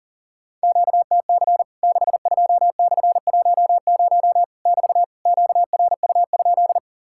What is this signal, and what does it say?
Morse code, heard as a keyed tone, about 40 wpm, 700 Hz. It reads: QTC 62/10 <BT> QRU?
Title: Worked All Europe DX Contest - QTC Exchange - 40wpm